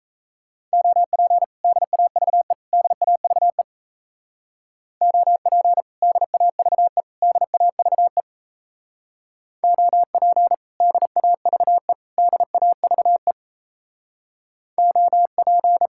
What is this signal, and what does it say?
Morse code, heard as a keyed tone, about 35 wpm, 700 Hz. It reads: OP DAVE DAVE OP DAVE DAVE OP DAVE DAVE OP
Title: Speed-Racing Common QSO Phrases - 28wpm